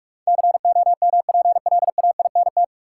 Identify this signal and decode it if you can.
Morse code, heard as a keyed tone, about 41 wpm, 700 Hz. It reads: COMPLAINT
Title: Sentences from Top 1400 Words - New Word Drill - 45wpm